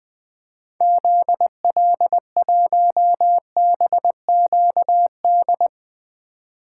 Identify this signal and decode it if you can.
Morse code, heard as a keyed tone, about 20 wpm, 700 Hz. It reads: ZL1BQD
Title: 2023 POTA Activator Call Signs - 20wpm